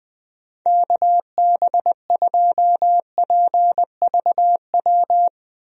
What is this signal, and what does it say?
Morse code, heard as a keyed tone, about 20 wpm, 700 Hz. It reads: KB2PVW
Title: USA Call Signs - 6 Characters - Repeated 2x - 20wpm